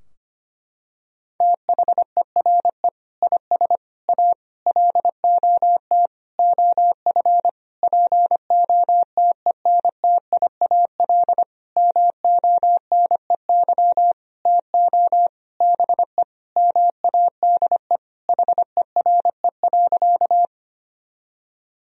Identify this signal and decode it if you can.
Morse code, heard as a keyed tone, about 25 wpm, 700 Hz. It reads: THERE IS A LOT OF POTENTIAL MONEY TO BE MADE HERE.
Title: Sentences from Top 1000 Words - Review Part 4 - 25wpm